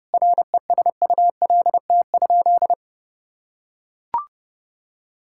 Morse code, heard as a keyed tone, about 30 wpm, 700 Hz. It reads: RESULT?
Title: Sentences from Top 1000 Words - Review Part 5 - 30wpm